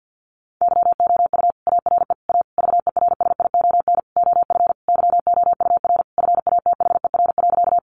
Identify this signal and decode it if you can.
Morse code, heard as a keyed tone, about 48 wpm, 700 Hz. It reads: YOU ARE A VERSION OF YOUR FATHER.
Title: Sentences from Top 1400 Words 50wpm